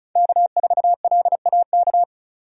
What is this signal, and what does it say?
Morse code, heard as a keyed tone, about 36 wpm, 700 Hz. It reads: K4LAK
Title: USA Call Signs - 5 Characters - 35wpm